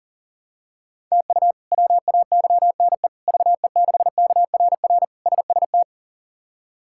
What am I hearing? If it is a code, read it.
Morse code, heard as a keyed tone, about 40 wpm, 700 Hz. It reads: TU WAYNE VE6KRR SST
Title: K1USN SST QSO Practice - 40wpm